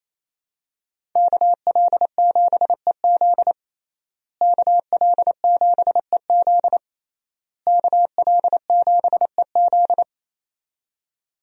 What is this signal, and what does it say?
Morse code, heard as a keyed tone, about 28 wpm, 700 Hz. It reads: KL7EZ KL7EZ KL7EZ
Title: DXCC-100 Call Signs - 5 Characters - Repeated 3x - 28wpm